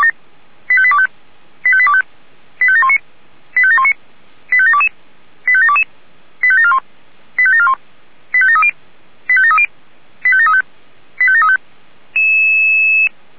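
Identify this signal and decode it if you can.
Five-tone selective call (ZVEI), heard as quick runs of five tones, tones of about 70 ms. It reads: 85825 85825 85719 85719 85722 85722 85741 85741 85730 85730 85735 85735
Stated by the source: ZVEI five-tone call 85728 received by SDR.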